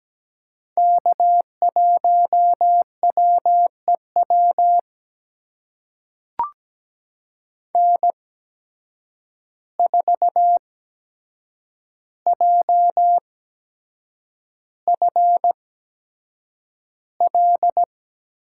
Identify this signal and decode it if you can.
Morse code, heard as a keyed tone, about 17 wpm, spaced as for 7 wpm, 700 Hz. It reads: K1WEW N4JFL
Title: USA Call Signs - 5 Characters - 8x Character Spacing - 17wpm